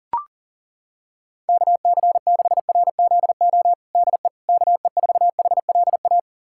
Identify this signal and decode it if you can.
Morse code, heard as a keyed tone, about 40 wpm, 700 Hz. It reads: KC6RZO DE KE4HLA
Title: USA Callsign DE Callsign - 6 Characters - 40wpm